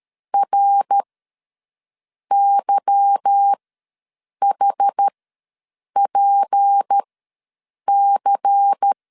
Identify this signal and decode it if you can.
Morse code, heard as a keyed tone, about 13 wpm, spaced as for 7 wpm, 800 Hz. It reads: RYHPC
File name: LEZIONI / 8 YJ45 / 8 Y ws 12wpm 3409